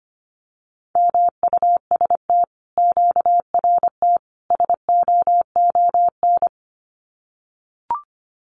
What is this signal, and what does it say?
Morse code, heard as a keyed tone, about 25 wpm, 700 Hz. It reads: MUST QRT SOON
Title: Common QSO Phrases Encore 25wpm